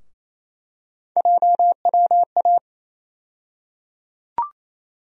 Morse code, heard as a keyed tone, about 28 wpm, 700 Hz. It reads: JWA
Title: USA Call Sign Postfixes - 3 Letters 28wpm